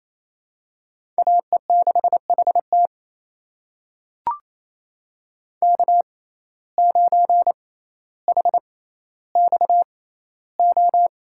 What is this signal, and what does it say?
Morse code, heard as a keyed tone, about 28 wpm, spaced as for 14 wpm, 700 Hz. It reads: AE6HT K9HXO
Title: USA Call Signs - 6x Character Spacing - 28wpm